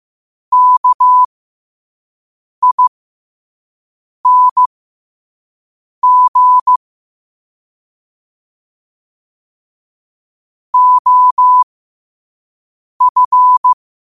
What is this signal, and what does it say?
Morse code, heard as a keyed tone, about 15 wpm, spaced as for 5 wpm, 1000 Hz. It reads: KING OF
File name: qotdfiles / 5wpm